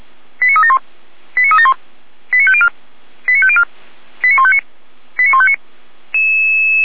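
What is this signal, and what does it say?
Five-tone selective call (ZVEI), heard as quick runs of five tones, tones of about 70 ms. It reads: 89371 89371 89504 89504 89159 89159
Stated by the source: ZVEI five-tone call received on an SDR, from the feuerwehr-orlamuende.org fire brigade audio archive.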